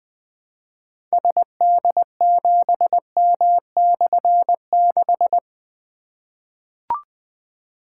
Morse code, heard as a keyed tone, about 20 wpm, 700 Hz. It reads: SD7M/6
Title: Difficult Call Signs 20wpm